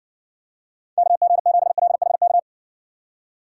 Morse code, heard as a keyed tone, about 60 wpm, 700 Hz. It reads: KG7LHK